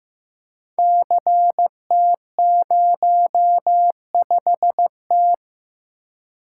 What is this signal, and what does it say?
Morse code, heard as a keyed tone, about 15 wpm, 700 Hz. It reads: CT05T
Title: Code Groups - 5 Letter:Numbers - 15wpm